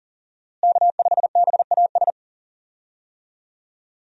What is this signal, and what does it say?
Morse code, heard as a keyed tone, about 40 wpm, 700 Hz. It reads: K5BAS
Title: USA Call Signs - 5 Characters - Repeated 3x - 40wpm